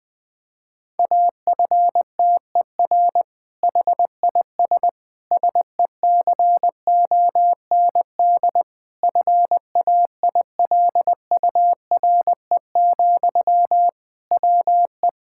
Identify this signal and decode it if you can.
Morse code, heard as a keyed tone, about 20 wpm, 700 Hz. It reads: AFTER HIS SECOND FAILURE, WE
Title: Sentences from Top 1000 Words - Review Part 6 - 20wpm